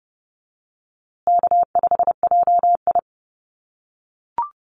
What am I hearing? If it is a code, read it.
Morse code, heard as a keyed tone, about 30 wpm, 700 Hz. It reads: K5JI
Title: USA Call Signs Repeated 3x 30wpm